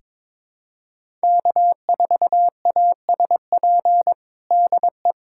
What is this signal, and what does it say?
Morse code, heard as a keyed tone, about 22 wpm, 700 Hz. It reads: K4ASP DE
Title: USA Callsign DE Callsign - 5 Characters - 22wpm